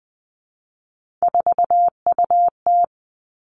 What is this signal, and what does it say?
Morse code, heard as a keyed tone, about 19 wpm, 700 Hz. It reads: 4UT
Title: Three Challenging Letter-Number Combinations 20wpm